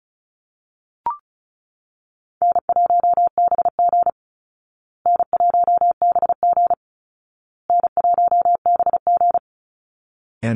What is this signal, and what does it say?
Morse code, heard as a keyed tone, about 34 wpm, 700 Hz. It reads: N1BG N1BG N1BG
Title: Easy Call Signs Repeated 3x - 35wpm